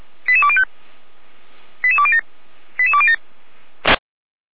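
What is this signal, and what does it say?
Five-tone selective call (ZVEI), heard as quick runs of five tones, tones of about 70 ms. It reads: 88205 88207 88207
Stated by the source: ZVEI five-tone selective call received on an SDR.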